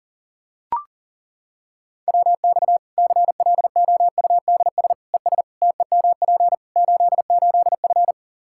Morse code, heard as a keyed tone, about 40 wpm, 700 Hz. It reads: WX CLOUDS ES TEMP 88F
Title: K1USN SST QSO Practice 40wpm